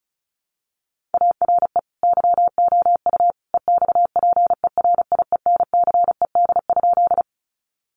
Morse code, heard as a keyed tone, about 34 wpm, 700 Hz. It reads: ARE YOU EXPERIENCED?
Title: Sentences from Top 1100 Words 35wpm